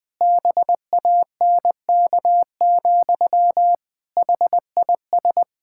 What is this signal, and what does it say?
Morse code, heard as a keyed tone, about 20 wpm, 700 Hz. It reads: BANK, HIS
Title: Pangram Challenge v2 - 20wpm